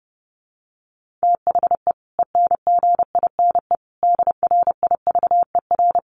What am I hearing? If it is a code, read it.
Morse code, heard as a keyed tone, about 30 wpm, 700 Hz. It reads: THE ENGINE DRIVER
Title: Sentences from Top 1500 Words 30wpm